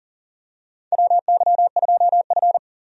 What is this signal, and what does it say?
Morse code, heard as a keyed tone, about 37 wpm, 700 Hz. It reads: WY2F
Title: USA Call Signs - 4 Characters - 40wpm